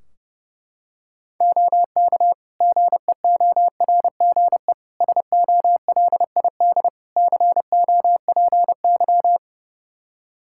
Morse code, heard as a keyed tone, about 30 wpm, 700 Hz. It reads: OK GEORGE SOLID COPY
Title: K1USN SST QSO Practice 30wpm